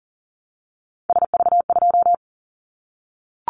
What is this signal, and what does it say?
Morse code, heard as a keyed tone, about 38 wpm, 700 Hz. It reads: SV2
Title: Callsign Prefixes for Top 50 DXCC - 40wpm